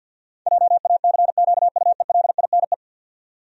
Morse code, heard as a keyed tone, about 47 wpm, 700 Hz. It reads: JACQUELINE
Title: USA Common Names - 50wpm